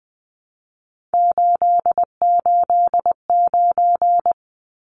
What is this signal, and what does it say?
Morse code, heard as a keyed tone, about 20 wpm, 700 Hz. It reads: 889
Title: Numbers - 3-Digit 20wpm